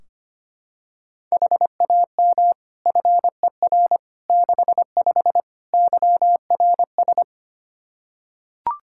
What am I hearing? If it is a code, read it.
Morse code, heard as a keyed tone, about 25 wpm, 700 Hz. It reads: HAM FER 65 YRS
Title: Practice QSO - Exchange 3 - More Info - 25wpm